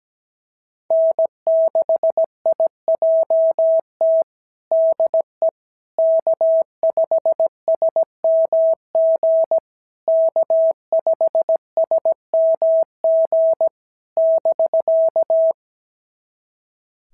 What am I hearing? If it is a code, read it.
Morse code, heard as a keyed tone, about 17 wpm, 650 Hz. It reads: N6IJT DE K5SMG K5SMG <BK>
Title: SOTA QSO Practice 17wpm